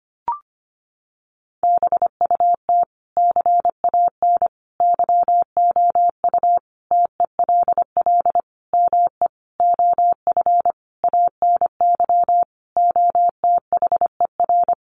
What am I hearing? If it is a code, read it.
Morse code, heard as a keyed tone, about 25 wpm, 700 Hz. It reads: BUT CAN YOU TELL ME OF ANY OTHER
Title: Sentences from Top 1400 Words 25wpm